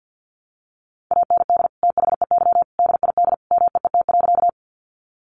Morse code, heard as a keyed tone, about 50 wpm, 700 Hz. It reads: AND THEY DID MEET.
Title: Sentences from Top 1100 Words 50wpm